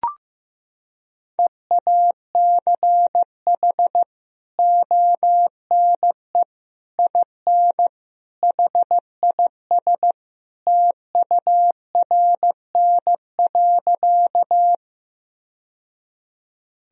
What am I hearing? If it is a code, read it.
Morse code, heard as a keyed tone, about 15 wpm, 700 Hz. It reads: EACH ONE IN HIS TURN.
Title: Sentences from Top 1000 Words - Review Part 1 - 15wpm